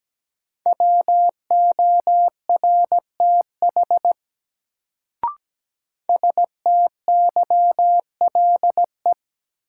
Morse code, heard as a keyed tone, about 17 wpm, 700 Hz. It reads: WORTH STYLE
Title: Sentences from Top 800 Words - New Word Drill - 17wpm